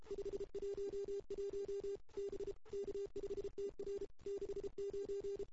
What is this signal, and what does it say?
Morse code, heard as a keyed tone, about 31 wpm, 410 Hz. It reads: 511 B K5TR 69
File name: CW / 2008-ss-cw-k5tr-op-n5rz -01:00